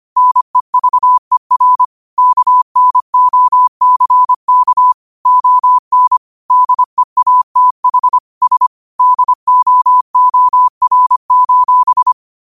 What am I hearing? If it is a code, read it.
Morse code, heard as a keyed tone, about 25 wpm, 1000 Hz. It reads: NEVER KNOCK ON DEATH S DOOR:
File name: qotdfiles / 25wpm